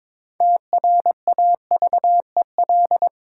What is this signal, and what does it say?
Morse code, heard as a keyed tone, about 22 wpm, 690 Hz. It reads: TRAVEL
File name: Difficult Words with Hints - 22wpm